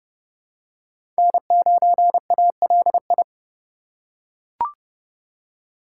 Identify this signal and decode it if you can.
Morse code, heard as a keyed tone, about 30 wpm, 700 Hz. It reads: N9ALI
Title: USA Call Signs - 3x Character Spacing - 30wpm